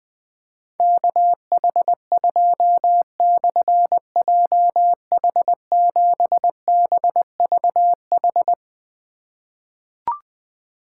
Morse code, heard as a keyed tone, about 20 wpm, 700 Hz. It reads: KH2/JH7BVH
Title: Difficult Call Signs 20wpm